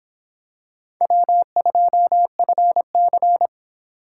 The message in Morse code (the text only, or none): W2FC